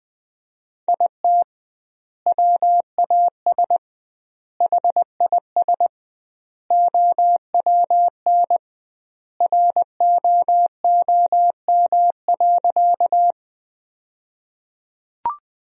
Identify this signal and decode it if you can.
Morse code, heard as a keyed tone, about 20 wpm, 700 Hz. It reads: IT WAS HIS OWN ROOM.
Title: 2x Word-Spacing - Sentences from Top 300 Words - 20wpm